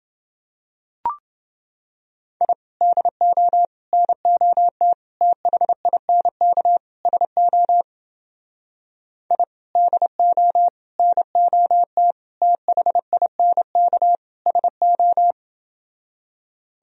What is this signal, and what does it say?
Morse code, heard as a keyed tone, about 29 wpm, 700 Hz. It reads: I DO NOT THINK SO I DO NOT THINK SO